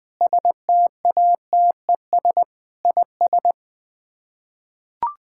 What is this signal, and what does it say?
Morse code, heard as a keyed tone, about 20 wpm, 700 Hz. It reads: STATES IS
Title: Build A Phrase - 5-Word Phrases - 20wpm